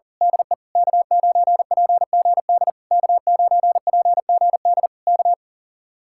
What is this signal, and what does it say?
Morse code, heard as a keyed tone, about 38 wpm, 700 Hz. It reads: DE K9PGD K9PGD K
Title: Speed-Racing Common QSO Phrases - 40wpm